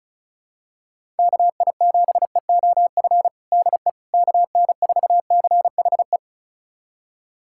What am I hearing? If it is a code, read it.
Morse code, heard as a keyed tone, about 34 wpm, 700 Hz. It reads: KI7EOF DE KN4CHE
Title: USA Callsign DE Callsign - 6 Characters - 35wpm